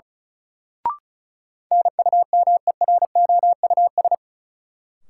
Morse code, heard as a keyed tone, about 36 wpm, 700 Hz. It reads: NUMEROUS
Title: Sentences from Top 1400 Words - New Word Drill - 35wpm